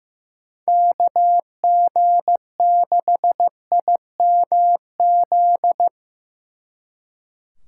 Morse code, heard as a keyed tone, about 17 wpm, 700 Hz. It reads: KG6IMZ
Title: Speed Racing - USA 2x3 Call Signs 15wpm